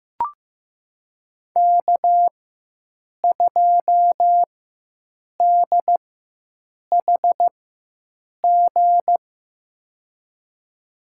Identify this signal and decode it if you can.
Morse code, heard as a keyed tone, about 15 wpm, spaced as for 7 wpm, 700 Hz. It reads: K2DHG
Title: USA Call Signs - 5 Characters - 4x Character Spacing - 15wpm